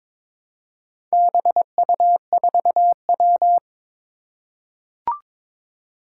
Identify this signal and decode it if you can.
Morse code, heard as a keyed tone, about 22 wpm, 700 Hz. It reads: BU4W
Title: Four Challenging Letter-Number Combinations 22wpm